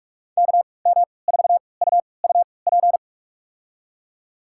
Morse code, heard as a keyed tone, about 45 wpm, spaced as for 25 wpm, 700 Hz. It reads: KM4UUP
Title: USA Call Signs - 3x Character Spacing - 45wpm